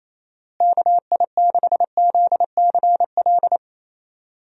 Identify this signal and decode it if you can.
Morse code, heard as a keyed tone, about 28 wpm, 700 Hz. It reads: KI6ZCL
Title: USA Call Signs - 6 Characters - Repeated 2x - 28wpm